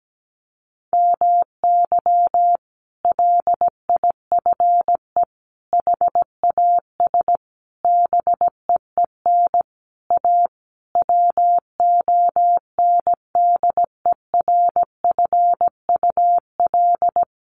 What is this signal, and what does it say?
Morse code, heard as a keyed tone, about 17 wpm, 700 Hz. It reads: MY LIFE HAS BEEN A WONDERFUL
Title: Sentences from Top 1100 Words 17wpm